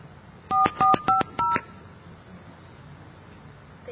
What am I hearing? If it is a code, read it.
DTMF, heard as telephone touch tones, tones of about 150 ms. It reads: 4450